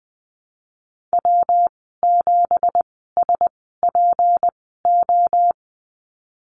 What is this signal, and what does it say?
Morse code, heard as a keyed tone, about 19 wpm, spaced as for 14 wpm, 700 Hz. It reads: W7SPO
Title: 179 Lesson 29g - Slash - Call Signs 20wpm